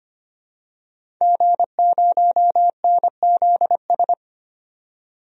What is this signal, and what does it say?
Morse code, heard as a keyed tone, about 25 wpm, 700 Hz. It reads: G0NZS
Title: DXCC-100 Call Signs - 5 Characters - 25wpm